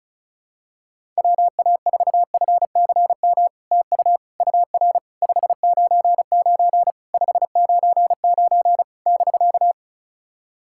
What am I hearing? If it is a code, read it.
Morse code, heard as a keyed tone, about 36 wpm, 700 Hz. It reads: WA4FCM TU UR 599 599 <BK>
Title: SOTA QSO Practice 35wpm